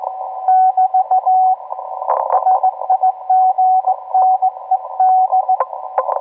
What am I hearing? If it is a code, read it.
Morse code, heard as a keyed tone, about 19 wpm, 750 Hz, with radio noise. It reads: <BT> SIGNED